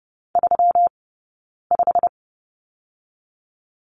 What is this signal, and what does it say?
Morse code, heard as a keyed tone, about 30 wpm, 700 Hz. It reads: 3 5
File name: Lesson 20d - Number 3 - Two Characters 30wpm